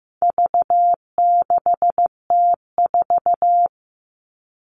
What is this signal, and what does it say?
Morse code, heard as a keyed tone, about 15 wpm, 700 Hz. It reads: V6T4
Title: All Cut Numbers - 4-Digit 15wpm